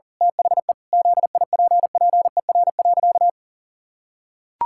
Morse code, heard as a keyed tone, about 37 wpm, 700 Hz. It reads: THE ZIPPER.
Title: Pangram Challenge v2 - 40wpm